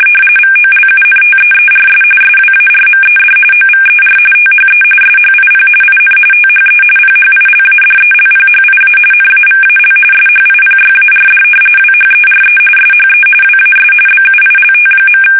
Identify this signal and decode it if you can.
RTTY, about 45.45 baud. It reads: VOLUNTEERS SWARMED OVER THE SHIP IN THE PHILADELPHIA NAVAL SHIPYARD WHERE SHE WAS BUILT.
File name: fsk850 / battleshipnj